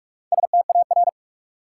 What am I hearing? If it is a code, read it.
Morse code, heard as a keyed tone, about 46 wpm, 700 Hz. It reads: STAR
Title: Sentences from Top 1000 Words - Review Part 2 - New Word Drill - 45wpm